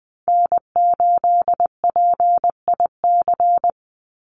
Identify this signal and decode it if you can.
Morse code, heard as a keyed tone, about 20 wpm, 700 Hz. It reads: N8PIC
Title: USA Call Signs 20wpm